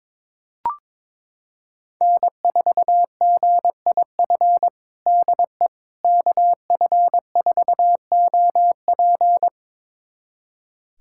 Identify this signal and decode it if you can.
Morse code, heard as a keyed tone, about 22 wpm, 700 Hz. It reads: N4GIF DE KF4OP
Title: USA Callsign DE Callsign - 5 Characters - 22wpm